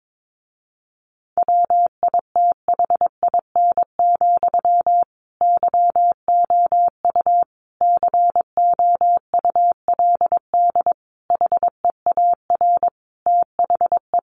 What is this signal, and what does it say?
Morse code, heard as a keyed tone, about 22 wpm, 700 Hz. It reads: WITHIN, YOU COULD HEAR THE
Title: A Sound Of Thunder 22wpm